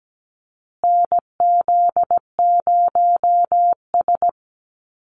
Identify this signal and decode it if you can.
Morse code, heard as a keyed tone, about 17 wpm, 700 Hz. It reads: NZ0S
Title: USA Call Signs Repeated 3x 17wpm